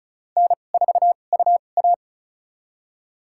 Morse code, heard as a keyed tone, about 36 wpm, spaced as for 25 wpm, 700 Hz. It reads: N4UA
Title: USA Call Signs - 2x Character Spacing - 35wpm